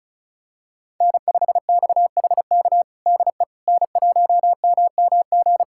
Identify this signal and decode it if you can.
Morse code, heard as a keyed tone, about 36 wpm, 700 Hz. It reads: N5XHK DE N1MMG
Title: USA Callsign DE Callsign - 5 Characters - 35wpm